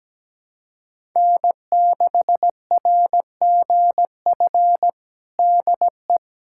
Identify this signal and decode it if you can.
Morse code, heard as a keyed tone, about 17 wpm, 700 Hz. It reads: N6RGF DE
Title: USA Callsign DE Callsign - 5 Characters - 17wpm